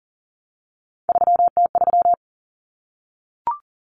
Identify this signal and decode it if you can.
Morse code, heard as a keyed tone, about 40 wpm, 700 Hz. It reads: 3T3
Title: Common Cut Numbers - 3-Digit 40wpm